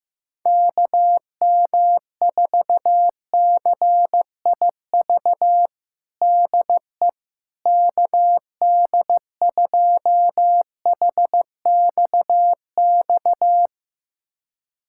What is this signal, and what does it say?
Morse code, heard as a keyed tone, about 15 wpm, 700 Hz. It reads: KM4CIV DE KD2HXX
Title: USA Callsign DE Callsign 15wpm